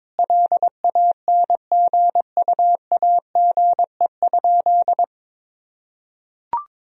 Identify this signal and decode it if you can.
Morse code, heard as a keyed tone, about 22 wpm, 700 Hz. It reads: LANGUAGE?
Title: Sentences from Top 1000 Words - Review Part 3 - 22wpm